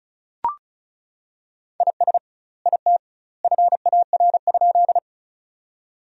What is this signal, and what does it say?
Morse code, heard as a keyed tone, about 36 wpm, 700 Hz. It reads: IS IT FAR?
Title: 2x Word-Spacing - Sentences from Top 300 Words - 35wpm